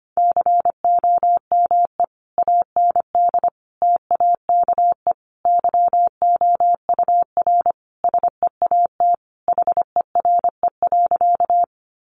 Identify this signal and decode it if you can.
Morse code, heard as a keyed tone, about 25 wpm, 700 Hz. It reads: COME AND TAKE YOUR SEAT HERE.
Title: Sentences from Top 1500 Words 25wpm